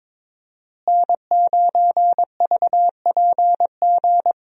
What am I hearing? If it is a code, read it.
Morse code, heard as a keyed tone, about 22 wpm, 700 Hz. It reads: N9VPG
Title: USA Call Signs - 5 Characters - Repeated 2x - 22wpm